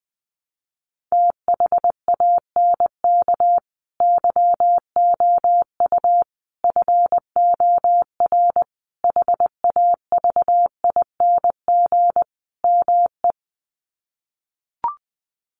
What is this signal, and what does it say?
Morse code, heard as a keyed tone, about 19 wpm, 700 Hz. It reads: THANK YOU FOR HAVING ME
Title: Sets of 5 Words Encore 20wpm